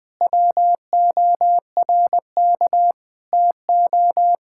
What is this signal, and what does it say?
Morse code, heard as a keyed tone, about 20 wpm, 700 Hz. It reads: WORK TO
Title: Sentences with 6-Word Phrases - Spoken Beginning - 20wpm